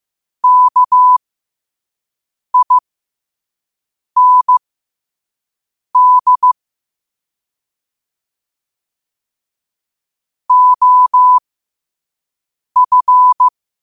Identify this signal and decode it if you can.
Morse code, heard as a keyed tone, about 15 wpm, spaced as for 5 wpm, 1000 Hz. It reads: KIND OF
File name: qotdfiles / 5wpm